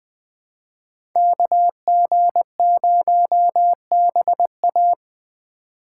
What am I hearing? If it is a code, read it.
Morse code, heard as a keyed tone, about 20 wpm, 700 Hz. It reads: KG0BA
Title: USA Call Signs - 2x Character Spacing - 20wpm